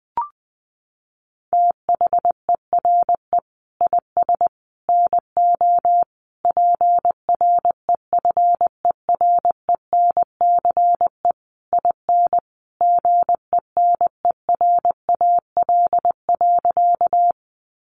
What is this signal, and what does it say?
Morse code, heard as a keyed tone, about 20 wpm, 700 Hz. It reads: THERE IS NO PREFERENCE IN GENERAL.
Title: Sentences from Top 1500 Words 20wpm